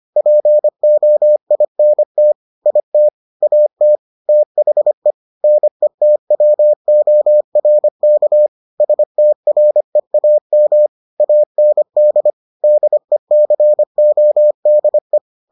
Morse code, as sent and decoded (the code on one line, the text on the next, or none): .--. --- .. -. - / .. - / .- - / - .... / -. - .-- --- .-. -.- / ... - .-. .- -- / .- -. -.. / -.. -.-. --- -.. .
POINT IT AT THE NETWORK STREAM AND DECODE